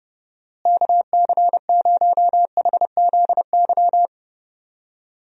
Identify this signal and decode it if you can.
Morse code, heard as a keyed tone, about 30 wpm, 700 Hz. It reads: KC0HZY
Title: USA Call Signs - 6 Characters - Repeated 2x - 30wpm